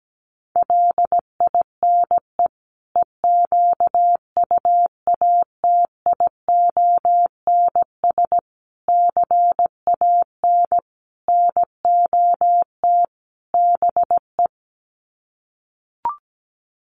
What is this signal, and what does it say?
Morse code, heard as a keyed tone, about 17 wpm, 700 Hz. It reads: LINE EQUATIONS CAN NOT BE
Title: Sets of 5 Words 17wpm